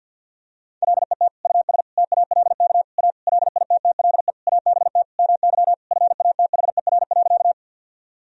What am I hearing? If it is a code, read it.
Morse code, heard as a keyed tone, about 51 wpm, 700 Hz. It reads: LET US TALK A LITTLE ABT MY FATHER.
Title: Sentences from Top 400 Words and Abbreviations 50wpm